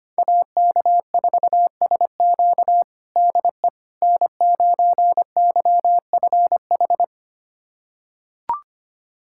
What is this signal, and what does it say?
Morse code, heard as a keyed tone, about 25 wpm, 700 Hz. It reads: AK4SQ DE N9YFH
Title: USA Callsign DE Callsign - 5 Characters - 25wpm